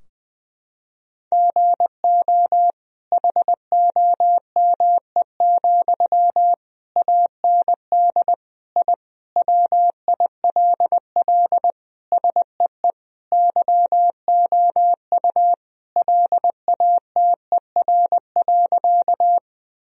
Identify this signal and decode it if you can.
Morse code, heard as a keyed tone, about 20 wpm, 700 Hz. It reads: GO HOME, AND I WILL SEE YOU LATER.
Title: Sentences from Top 1000 Words - Review Part 4 - 20wpm